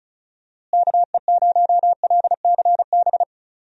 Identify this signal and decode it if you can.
Morse code, heard as a keyed tone, about 34 wpm, 700 Hz. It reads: KE0LCB
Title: USA Call Signs - 6 Characters - Repeated 3x - 35wpm